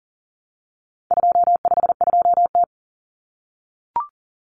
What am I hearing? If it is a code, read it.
Morse code, heard as a keyed tone, about 40 wpm, 700 Hz. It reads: 252T
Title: Common Cut Numbers - 4-Digit 40wpm